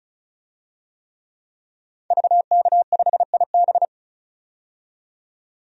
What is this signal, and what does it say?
Morse code, heard as a keyed tone, about 34 wpm, 700 Hz. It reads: VK5IB
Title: DXCC-100 Call Signs - 5 Characters - 35wpm